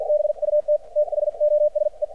Morse code, heard as a keyed tone, about 50 wpm, spaced as for 43 wpm, 600 Hz. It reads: BUT THOSE
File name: archived / CRSnet